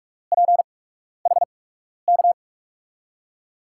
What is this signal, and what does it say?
Morse code, heard as a keyed tone, about 46 wpm, 700 Hz. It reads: P H K
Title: USA Call Signs - 8x Character Spacing - 45wpm